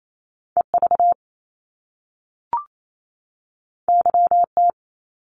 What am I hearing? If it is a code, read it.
Morse code, heard as a keyed tone, about 28 wpm, 700 Hz. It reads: EV YT
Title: Top Two Letter Combinations in Words 28wpm